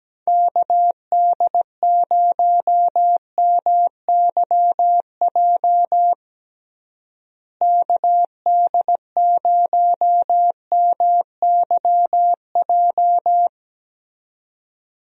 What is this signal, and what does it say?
Morse code, heard as a keyed tone, about 17 wpm, 700 Hz. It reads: KD0MYJ KD0MYJ
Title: USA Call Signs - 6 Characters - Repeated 2x - 17wpm